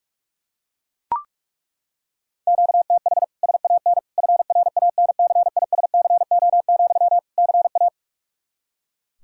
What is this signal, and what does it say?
Morse code, heard as a keyed tone, about 41 wpm, 700 Hz. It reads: QTH SAN FRANCISCO, CA
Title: Practice QSO - Exchange 1 - Beginning - 45wpm